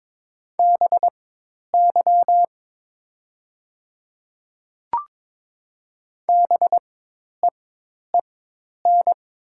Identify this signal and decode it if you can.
Morse code, heard as a keyed tone, about 22 wpm, spaced as for 10 wpm, 700 Hz. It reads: BY BEEN
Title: 4x Character Spacing - Top 300 Words - 22wpm